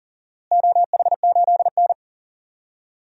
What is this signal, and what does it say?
Morse code, heard as a keyed tone, about 40 wpm, 700 Hz. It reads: OH8N
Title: DXCC-100 Call Signs - 4 Characters - Repeated 3x - 40wpm